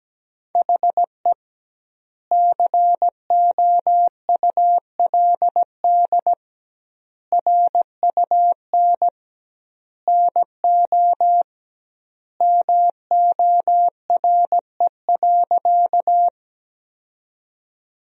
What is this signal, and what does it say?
Morse code, heard as a keyed tone, about 17 wpm, 700 Hz. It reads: HE COULD RUN NO MORE.
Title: 2x Word-Spacing - Sentences from Top 300 Words - 17wpm